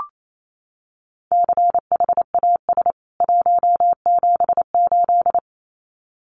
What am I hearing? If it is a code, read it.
Morse code, heard as a keyed tone, about 28 wpm, 700 Hz. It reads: CHAS 178
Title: CWT Exchange 28wpm